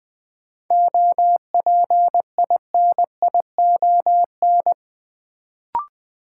Morse code, heard as a keyed tone, about 20 wpm, 700 Hz. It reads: OPINION
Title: Sentences from Top 1200 Words - New Word Drill - 20wpm